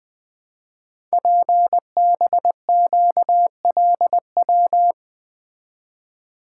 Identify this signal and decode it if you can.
Morse code, heard as a keyed tone, about 20 wpm, 700 Hz. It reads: PBQLW
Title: Code Groups - 5 Letters - 20wpm